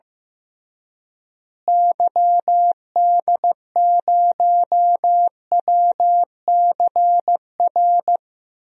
Morse code, heard as a keyed tone, about 15 wpm, 700 Hz. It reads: YD0WCR
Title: DXCC-100 Call Signs - 6 Characters - Repeated 3x - 15wpm